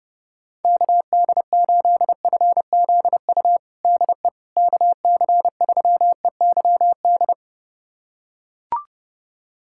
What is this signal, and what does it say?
Morse code, heard as a keyed tone, about 32 wpm, 700 Hz. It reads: KD8FZU DE KC3EYD
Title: USA Callsign DE Callsign 30wpm